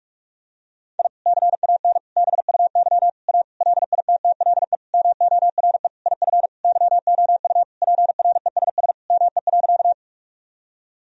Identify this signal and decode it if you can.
Morse code, heard as a keyed tone, about 41 wpm, 700 Hz. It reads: I CAN BUY A LITTLE MORE IF YOU PRESS ME.
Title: Sentences from Top 1000 Words - Review Part 1 - 45wpm